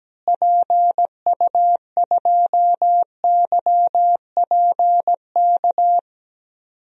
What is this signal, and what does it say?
Morse code, heard as a keyed tone, about 17 wpm, 700 Hz. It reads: PU2YPK
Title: DXCC-100 Call Signs - 6 Characters - Repeated 2x - 17wpm